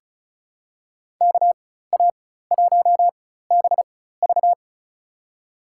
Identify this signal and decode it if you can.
Morse code, heard as a keyed tone, about 34 wpm, spaced as for 16 wpm, 700 Hz. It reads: KA1BV
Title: USA Call Signs - 5 Characters - 4x Character Spacing - 35wpm